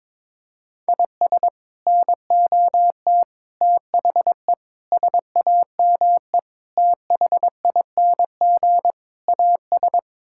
Morse code, heard as a keyed tone, about 22 wpm, 700 Hz. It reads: IS NOT THE SAME THING AS
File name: practice / 22wpm / Sets of 6 Words - 22wpm